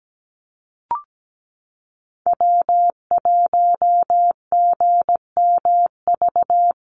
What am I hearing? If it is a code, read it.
Morse code, heard as a keyed tone, about 17 wpm, 700 Hz. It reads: W1GMV
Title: USA Call Signs Repeated 3x 17wpm